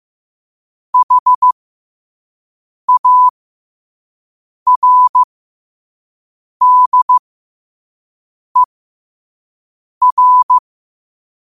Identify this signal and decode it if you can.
Morse code, heard as a keyed tone, about 15 wpm, spaced as for 5 wpm, 1000 Hz. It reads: HARDER
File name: qotdfiles / 5wpm